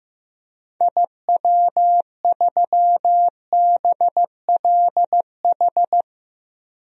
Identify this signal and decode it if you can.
Morse code, heard as a keyed tone, about 15 wpm, 700 Hz. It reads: IW3BLH
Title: DXCC-100 Call Signs - 6 Characters - Repeated 3x - 15wpm